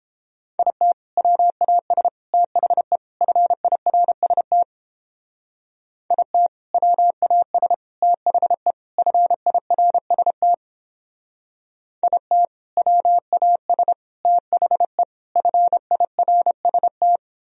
Morse code, heard as a keyed tone, about 29 wpm, 700 Hz. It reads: IT WAS THE FIRST IT WAS THE FIRST IT WAS THE FIRST